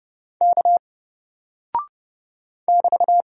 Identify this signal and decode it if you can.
Morse code, heard as a keyed tone, about 30 wpm, 700 Hz. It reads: K <BT>
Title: Single Character (Letter:Number:Punctuation:Prosign) 30wpm